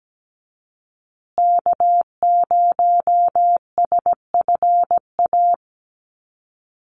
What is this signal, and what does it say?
Morse code, heard as a keyed tone, about 17 wpm, 700 Hz. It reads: K0SFA
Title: USA Call Signs Repeated 2x 17wpm